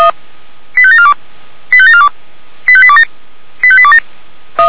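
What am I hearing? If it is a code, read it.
Five-tone selective call (ZVEI), heard as quick runs of five tones, tones of about 70 ms. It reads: 85742 85742 85827 85827
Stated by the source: Five-tone ZVEI selective call 85749, received on SDR by feuerwehr-orlamuende.org fire brigade server.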